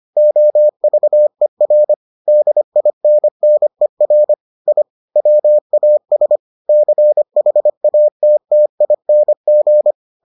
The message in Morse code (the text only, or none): OVER DINNER I WAS CHATTING